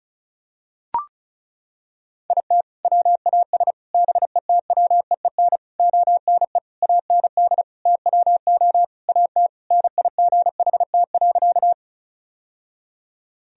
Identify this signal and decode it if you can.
Morse code, heard as a keyed tone, about 34 wpm, 700 Hz. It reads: IT WAS BETWEEN ONE AND TWO AT NIGHT.
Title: Sentences from Top 1000 Words - Review Part 1 - 35wpm